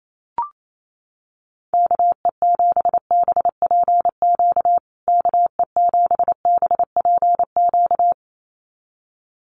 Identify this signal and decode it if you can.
Morse code, heard as a keyed tone, about 28 wpm, 700 Hz. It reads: KE7BPQ KE7BPQ
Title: USA Call Signs Repeated 2x 28wpm